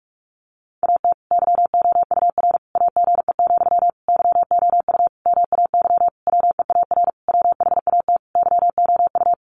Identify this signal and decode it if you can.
Morse code, heard as a keyed tone, about 44 wpm, 700 Hz. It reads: AT YOUR AGE, YOU MAY WEAR WHAT YOU
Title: Sentences from Top 1500 Words 45wpm